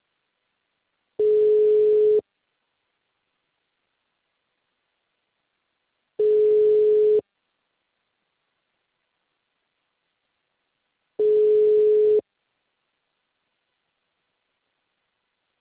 Ringback tone (Europe, 425 Hz).
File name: out- -2018- - - .73655